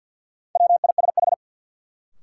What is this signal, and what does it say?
Morse code, heard as a keyed tone, about 53 wpm, 700 Hz. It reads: WISH
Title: Sentences from Top 1600 Words - New Word Drill - 50wpm